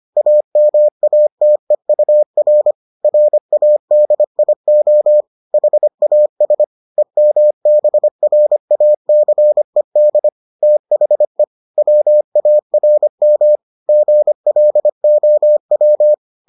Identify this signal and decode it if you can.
Morse code, heard as a keyed tone, about 25 wpm, 600 Hz. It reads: AMATEUR RADIO HAS EMBRACED THE WARM GLOW